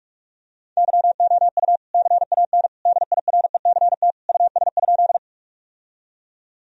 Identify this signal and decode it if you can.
Morse code, heard as a keyed tone, about 46 wpm, 700 Hz. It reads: YOU CAN DIRECT US?
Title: Sentences from Top 1000 Words - Review Part 2 - 45wpm